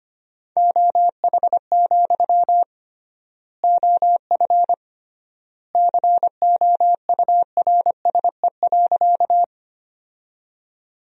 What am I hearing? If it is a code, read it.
Morse code, heard as a keyed tone, about 25 wpm, 700 Hz. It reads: OH, OF COURSE.
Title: 3x Word-Spacing - Sentences from Top 500 Words - 25wpm